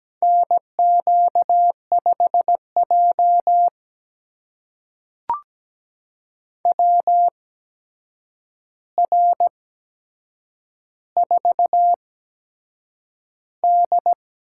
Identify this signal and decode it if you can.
Morse code, heard as a keyed tone, about 17 wpm, spaced as for 7 wpm, 700 Hz. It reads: NQ5J WR4D
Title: USA Call Signs - 4 Characters - 8x Character Spacing - 17wpm